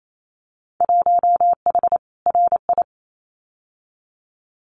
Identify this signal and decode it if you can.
Morse code, heard as a keyed tone, about 28 wpm, 700 Hz. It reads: 1H RI
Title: Winter Field Day Exchanges 28wpm